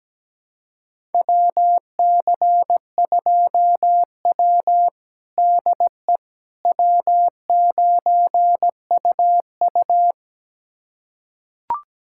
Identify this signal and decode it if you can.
Morse code, heard as a keyed tone, about 17 wpm, 700 Hz. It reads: WC2W DE W9UU
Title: USA Callsign DE Callsign - 4 Characters - 17wpm